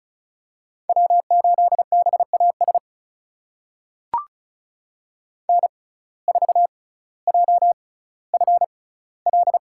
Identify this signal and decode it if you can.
Morse code, heard as a keyed tone, about 36 wpm, spaced as for 18 wpm, 700 Hz. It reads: W8BAS N4JFL
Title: USA Call Signs - 5 Characters - 6x Character Spacing - 35wpm